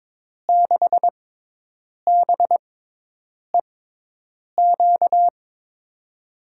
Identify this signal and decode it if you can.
Morse code, heard as a keyed tone, about 22 wpm, spaced as for 8 wpm, 700 Hz. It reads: 6BEQ
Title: USA Call Signs - 6x Character Spacing - 22wpm